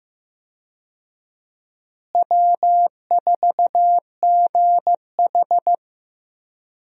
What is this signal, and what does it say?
Morse code, heard as a keyed tone, about 15 wpm, 700 Hz. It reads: W4GH
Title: USA Call Signs - 4 Characters - Repeated 3x - 15wpm